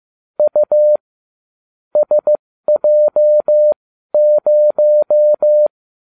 Morse code, heard as a keyed tone, about 15 wpm, spaced as for 12 wpm, 600 Hz. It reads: U SJ0